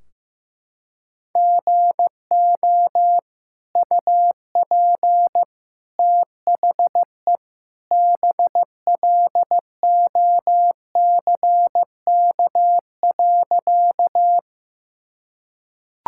Morse code, heard as a keyed tone, about 15 wpm, 700 Hz. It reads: GO UP THE BLOCK.
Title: Sentences from Top 1600 Words 15wpm